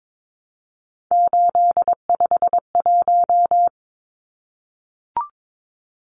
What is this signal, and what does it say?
Morse code, heard as a keyed tone, about 22 wpm, 700 Hz. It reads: 851
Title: Numbers - 3-Digit 22wpm